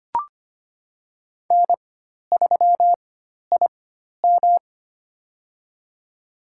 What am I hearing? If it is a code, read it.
Morse code, heard as a keyed tone, about 25 wpm, spaced as for 12 wpm, 700 Hz. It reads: N3IM
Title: USA Call Signs - 4 Characters - 4x Character Spacing - 25wpm